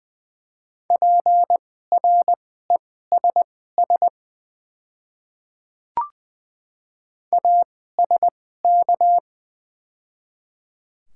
Morse code, heard as a keyed tone, about 20 wpm, spaced as for 14 wpm, 700 Hz. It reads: PRESS ASK
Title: 2x Character Spacing - Top 300 Words - 20wpm